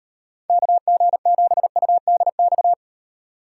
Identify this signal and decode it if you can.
Morse code, heard as a keyed tone, about 38 wpm, 700 Hz. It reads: KG7UDX